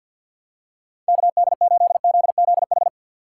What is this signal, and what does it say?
Morse code, heard as a keyed tone, about 53 wpm, 700 Hz. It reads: KD8ZZH
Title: USA Call Signs - 6 Characters - 4x Character Spacing - 50wpm